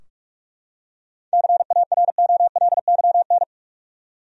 Morse code, heard as a keyed tone, about 46 wpm, 700 Hz. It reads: CAROLYN
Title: USA Common Names - 45wpm